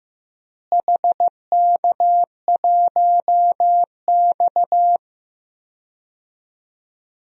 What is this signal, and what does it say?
Morse code, heard as a keyed tone, about 15 wpm, 700 Hz. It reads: HK1X
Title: DXCC-100 Call Signs - 4 Characters - Repeated 3x - 15wpm